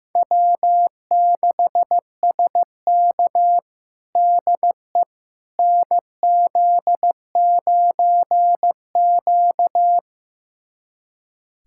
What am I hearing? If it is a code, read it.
Morse code, heard as a keyed tone, about 16 wpm, 700 Hz. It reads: W6SK DE NZ9Q
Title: USA Callsign DE Callsign - 4 Characters - 15wpm